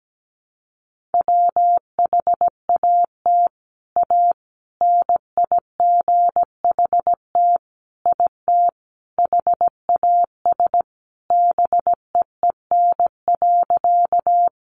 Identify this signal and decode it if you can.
Morse code, heard as a keyed tone, about 17 wpm, 700 Hz. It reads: WHAT A NIGHT IT HAS BEEN.
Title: Sentences from Top 300 Words 17wpm